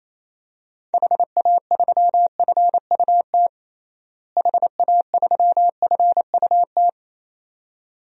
Morse code, heard as a keyed tone, about 28 wpm, 700 Hz. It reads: HA3FUT HA3FUT
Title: DXCC-100 Call Signs - 6 Characters - Repeated 2x - 28wpm